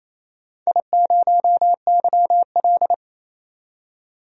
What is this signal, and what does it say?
Morse code, heard as a keyed tone, about 29 wpm, 700 Hz. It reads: I0YL
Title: DXCC-100 Call Signs - 4 Characters - 28wpm